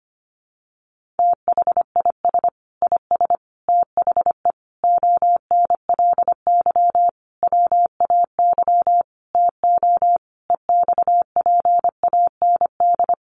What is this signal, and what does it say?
Morse code, heard as a keyed tone, about 25 wpm, 700 Hz. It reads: THIS IS THE ONLY WAY TO EXPAND
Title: Sentences from Top 1400 Words 25wpm